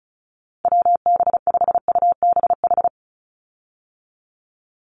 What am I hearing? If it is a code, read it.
Morse code, heard as a keyed tone, about 34 wpm, 700 Hz. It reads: WB5UBH
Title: USA Call Signs 35wpm